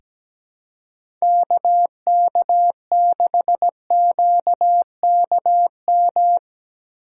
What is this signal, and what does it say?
Morse code, heard as a keyed tone, about 17 wpm, 700 Hz. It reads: KK6QKM
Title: USA Call Signs - 6 Characters - 2x Character Spacing - 17wpm